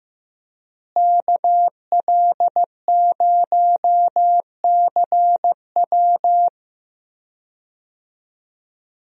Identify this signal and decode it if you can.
Morse code, heard as a keyed tone, about 15 wpm, 700 Hz. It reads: KL0CW
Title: USA Call Signs - 5 Characters - Repeated 3x - 15wpm